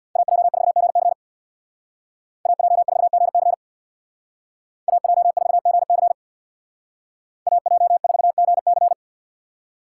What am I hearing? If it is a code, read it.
Morse code, heard as a keyed tone, about 64 wpm, 700 Hz. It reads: AJ4GC AJ4GC AJ4GC AJ4GC